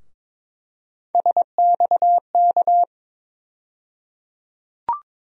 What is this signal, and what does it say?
Morse code, heard as a keyed tone, about 22 wpm, 700 Hz. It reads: SXK
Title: USA Call Sign Postfixes - 3 Letters 22wpm